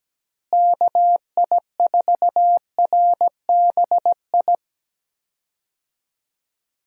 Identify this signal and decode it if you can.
Morse code, heard as a keyed tone, about 17 wpm, 700 Hz. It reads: KI4RBI
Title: USA Call Signs - 6 Characters - Repeated 2x - 17wpm